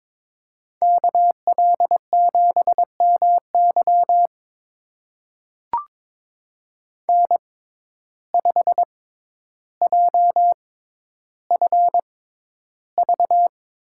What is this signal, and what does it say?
Morse code, heard as a keyed tone, about 22 wpm, spaced as for 11 wpm, 700 Hz. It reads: KL7MY N5JFV
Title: USA Call Signs - 5 Characters - 6x Character Spacing - 22wpm